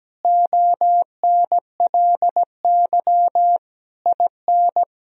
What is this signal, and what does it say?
Morse code, heard as a keyed tone, about 17 wpm, 700 Hz. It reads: ONLY IN
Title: Sentences from Top 1000 Words - Review Part 2 - 17wpm